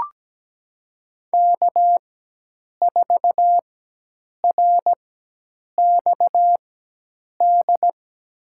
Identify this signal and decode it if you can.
Morse code, heard as a keyed tone, about 17 wpm, spaced as for 8 wpm, 700 Hz. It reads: K4RXD
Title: USA Call Signs - 4x Character Spacing - 17wpm